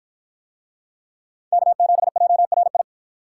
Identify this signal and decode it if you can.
Morse code, heard as a keyed tone, about 52 wpm, 700 Hz. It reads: K7JRI